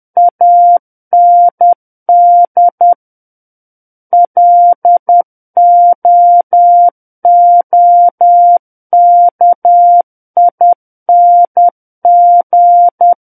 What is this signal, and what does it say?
Morse code, heard as a keyed tone, about 10 wpm, 700 Hz. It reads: AND LOOKING
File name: Book One / 01-Chapter 1 - 10 WPM